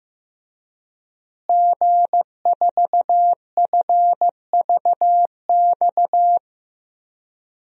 Morse code, heard as a keyed tone, about 15 wpm, 700 Hz. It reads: G4FVX
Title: DXCC-100 Call Signs - 5 Characters - Repeated 2x - 15wpm